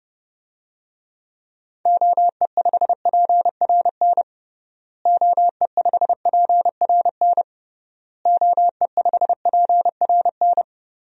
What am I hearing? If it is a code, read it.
Morse code, heard as a keyed tone, about 32 wpm, 700 Hz. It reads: OE5PRN OE5PRN OE5PRN
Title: DXCC-100 Call Signs - 6 Characters - Repeated 3x - 30wpm